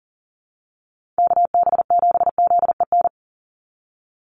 Morse code, heard as a keyed tone, about 40 wpm, 700 Hz. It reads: KB7ZEN
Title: USA Call Signs 40wpm